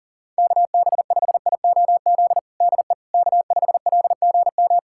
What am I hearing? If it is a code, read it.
Morse code, heard as a keyed tone, about 40 wpm, 700 Hz. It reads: KB5IOZ DE K5LGM
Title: USA Callsign DE Callsign 40wpm